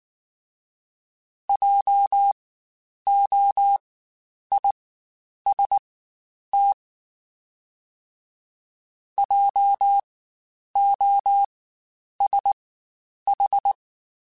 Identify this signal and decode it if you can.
Morse code, heard as a keyed tone, about 19 wpm, spaced as for 9 wpm, 800 Hz. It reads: JOIST JOSH
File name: Spaced Alphas / Spaced Alphas mp3s / Js sp @18